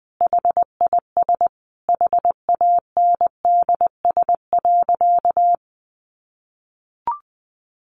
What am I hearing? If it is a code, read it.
Morse code, heard as a keyed tone, about 20 wpm, 700 Hz. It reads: HIS HANDS.
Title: A Sound Of Thunder 20wpm